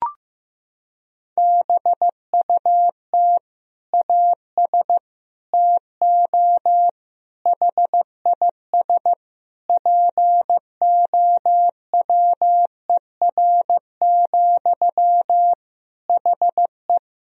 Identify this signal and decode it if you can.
Morse code, heard as a keyed tone, about 15 wpm, 700 Hz. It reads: BUT AS TO HIS POWER, HE
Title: Sentences from Top 1000 Words - Review Part 2 - 15wpm